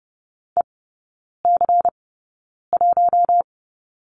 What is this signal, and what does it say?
Morse code, heard as a keyed tone, about 30 wpm, spaced as for 9 wpm, 700 Hz. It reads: EC1
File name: Lesson 16e - Letter c - Three Characters 30wpm